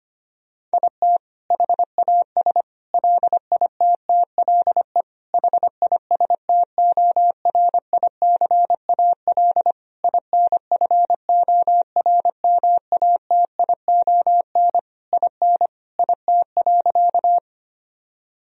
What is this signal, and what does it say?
Morse code, heard as a keyed tone, about 25 wpm, 700 Hz. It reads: IT HAS LITTLE HISTORICAL INFORMATION IN IT.
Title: Sentences from Top 1000 Words - Review Part 4 - 25wpm